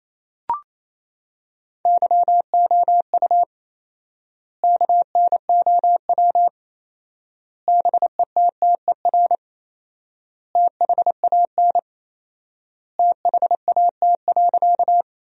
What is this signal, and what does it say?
Morse code, heard as a keyed tone, about 28 wpm, 700 Hz. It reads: YOU KNOW BETTER THAN THAT.
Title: 4x Word-Spacing - Sentences from Top 400 Words - 28wpm